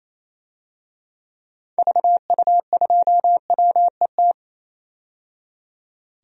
Morse code, heard as a keyed tone, about 28 wpm, 700 Hz. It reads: VU2WET
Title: DXCC-100 Call Signs - 6 Characters - 28wpm